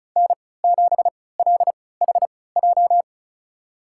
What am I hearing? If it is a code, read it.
Morse code, heard as a keyed tone, about 36 wpm, spaced as for 20 wpm, 700 Hz. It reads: N7LHJ
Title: USA Call Signs - 5 Characters - 3x Character Spacing - 35wpm